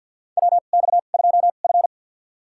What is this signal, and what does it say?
Morse code, heard as a keyed tone, about 53 wpm, spaced as for 37 wpm, 700 Hz. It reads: WX2F
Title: USA Call Signs - 2x Character Spacing - 50wpm